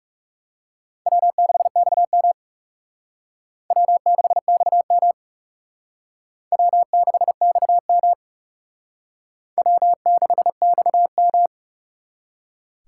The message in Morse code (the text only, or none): W6XM W6XM W6XM W6XM